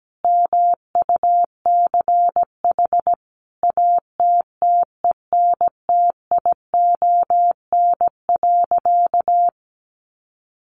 Morse code, heard as a keyed tone, about 17 wpm, 700 Hz. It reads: MUCH ATTENTION.
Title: Sentences from Top 1000 Words 17wpm